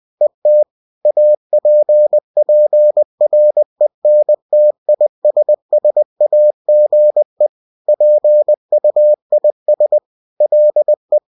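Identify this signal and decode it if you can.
Morse code, heard as a keyed tone, about 20 wpm, 600 Hz. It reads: ET APPRENTISSAGE PUIS LE
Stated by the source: UFT bulletin sent in CW.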